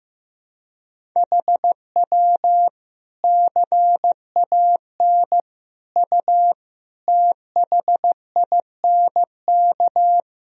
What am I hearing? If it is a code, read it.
Morse code, heard as a keyed tone, about 15 wpm, 700 Hz. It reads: HW CAN U THINK
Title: Sentences from Top 200 Words and Abbreviations 15wpm